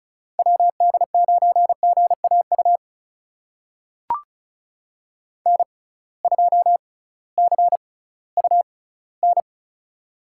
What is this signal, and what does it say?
Morse code, heard as a keyed tone, about 35 wpm, spaced as for 19 wpm, 700 Hz. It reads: WD9GAU N2CUN
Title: USA Call Signs - 6x Character Spacing - 35wpm